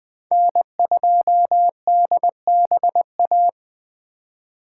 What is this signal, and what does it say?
Morse code, heard as a keyed tone, about 20 wpm, 700 Hz. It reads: N2DBA
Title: Speed Racing - USA 1x3 Call Signs 17wpm